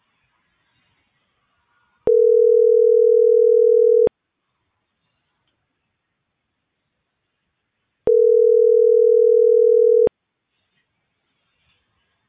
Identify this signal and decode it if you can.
Ringback tone (North America), 440 + 480 Hz.